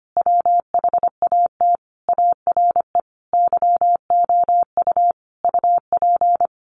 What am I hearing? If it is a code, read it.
Morse code, heard as a keyed tone, about 25 wpm, 700 Hz. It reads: WHAT ARE YOU UP
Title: Sentences from Top 100 Words 25wpm